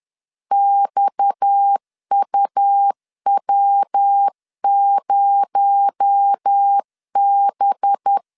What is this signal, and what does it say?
Morse code, heard as a keyed tone, about 10 wpm, 800 Hz. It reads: XUW0B
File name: LEZIONI / 6 PKX0 / 6 PKX0 10wpm 2707